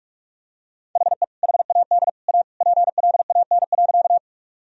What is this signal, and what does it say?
Morse code, heard as a keyed tone, about 46 wpm, 700 Hz. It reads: HE HAD A PLAN.
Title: Sentences from Top 1000 Words - Review Part 2 - 45wpm